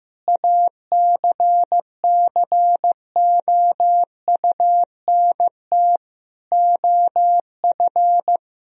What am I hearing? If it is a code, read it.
Morse code, heard as a keyed tone, about 15 wpm, 700 Hz. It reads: ACCOUNT OF
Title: Sentences from Top 1000 Words - Review Part 3 - 15wpm